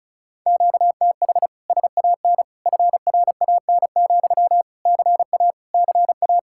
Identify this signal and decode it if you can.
Morse code, heard as a keyed tone, about 39 wpm, 700 Hz. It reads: QTH SAN FRAN, CA CA